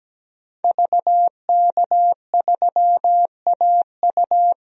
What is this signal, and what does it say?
Morse code, heard as a keyed tone, about 17 wpm, 700 Hz. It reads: VK3AU
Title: DXCC-100 Call Signs - 5 Characters - 17wpm